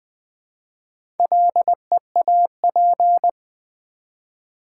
Morse code, heard as a keyed tone, about 20 wpm, 700 Hz. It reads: LEAP
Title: Build A Word - 1 Letter At A Time - 4-Letter Words - 20wpm